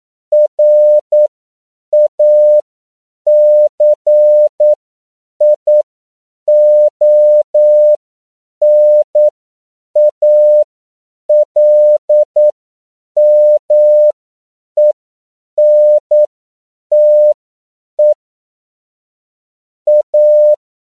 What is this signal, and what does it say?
Morse code, heard as a keyed tone, about 9 wpm, spaced as for 7 wpm, 600 Hz. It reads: RACIONALMENTE A
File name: CW / 10PPM